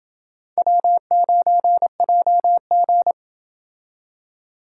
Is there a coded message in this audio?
Yes: Morse code from 0.6 to 3.1 s, heard as a keyed tone, about 28 wpm, 700 Hz. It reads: W9JG